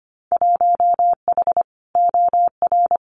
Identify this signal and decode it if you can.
Morse code, heard as a keyed tone, about 25 wpm, 700 Hz. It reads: 1H OR
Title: Winter Field Day Exchanges 25wpm